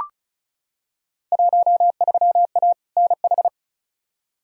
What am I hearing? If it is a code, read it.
Morse code, heard as a keyed tone, about 34 wpm, 700 Hz. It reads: 13A NH
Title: Field Day Exchanges Encore - 35wpm